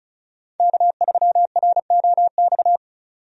Morse code, heard as a keyed tone, about 36 wpm, 700 Hz. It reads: K3ROX
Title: USA Call Signs - 5 Characters - 35wpm